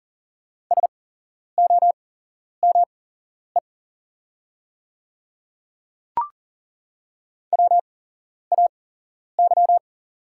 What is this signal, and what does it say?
Morse code, heard as a keyed tone, about 40 wpm, spaced as for 11 wpm, 700 Hz. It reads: SOME WAY
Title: 8x Character Spacing - Top 100 Words - 40wpm